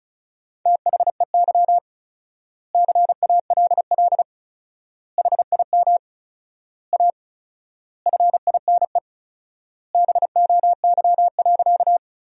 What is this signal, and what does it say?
Morse code, heard as a keyed tone, about 34 wpm, 700 Hz. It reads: THEY CALL HIM A FINE BOY.
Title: 4x Word-Spacing - Sentences from Top 500 Words - 35wpm